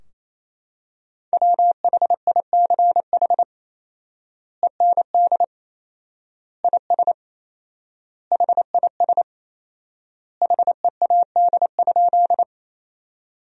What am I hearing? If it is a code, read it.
Morse code, heard as a keyed tone, about 27 wpm, 700 Hz. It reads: WHICH END IS HIS HEAD?
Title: 4x Word-Spacing - Sentences from Top 300 Words - 28wpm